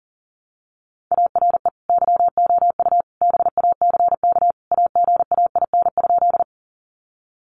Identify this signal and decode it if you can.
Morse code, heard as a keyed tone, about 40 wpm, 700 Hz. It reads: ARE YOU BACK AGAIN?
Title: Sentences from Top 200 Words 40wpm